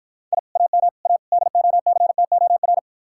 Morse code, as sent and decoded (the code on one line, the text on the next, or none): .. / .- -- / .- / -.. --- -.-. - --- .-.
I AM A DOCTOR